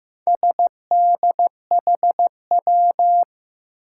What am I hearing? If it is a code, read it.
Morse code, heard as a keyed tone, about 15 wpm, 700 Hz. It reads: SDHW
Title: Four Challenging Letter-Number Combinations 15wpm